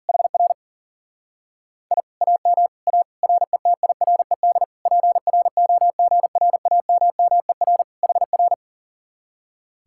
Morse code, heard as a keyed tone, about 52 wpm, 700 Hz. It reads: HR I AM A RETIRED PROGRAMMER HR